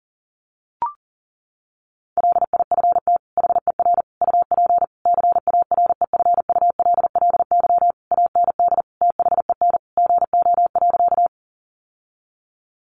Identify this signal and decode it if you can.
Morse code, heard as a keyed tone, about 40 wpm, 700 Hz. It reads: LIFT HER UP CAREFULLY AND THEN GO.
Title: Sentences from Top 1300 Words 40wpm